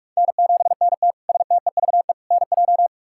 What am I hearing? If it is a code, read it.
Morse code, heard as a keyed tone, about 45 wpm, 700 Hz. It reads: N7NT STEVE NJ
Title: K1USN SST QSO Practice - 45wpm